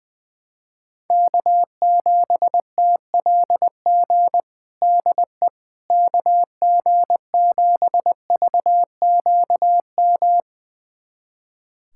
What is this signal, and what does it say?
Morse code, heard as a keyed tone, about 20 wpm, 700 Hz. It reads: K7TLG DE KG7VQM
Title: USA Callsign DE Callsign 20wpm